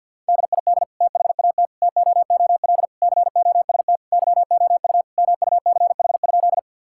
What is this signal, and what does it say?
Morse code, heard as a keyed tone, about 50 wpm, 700 Hz. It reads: DID THAT TOOL COST YOU MUCH?
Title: Sentences from Top 1000 Words - Review Part 6 - 50wpm